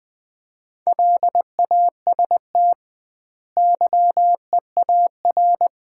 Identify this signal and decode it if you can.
Morse code, heard as a keyed tone, about 20 wpm, 700 Hz. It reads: LAST YEAR
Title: 2x Word-Spacing - Sets of 2 Words - 20wpm